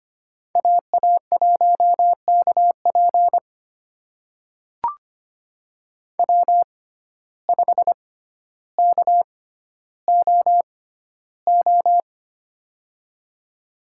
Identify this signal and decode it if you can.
Morse code, heard as a keyed tone, about 25 wpm, spaced as for 13 wpm, 700 Hz. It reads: AA1KP W5KOO
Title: USA Call Signs - 6x Character Spacing - 25wpm